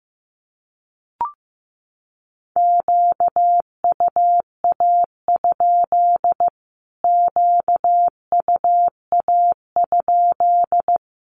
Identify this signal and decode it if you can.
Morse code, heard as a keyed tone, about 15 wpm, 700 Hz. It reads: QUA? QUA?
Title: Q-Codes Repeated 3x 15wpm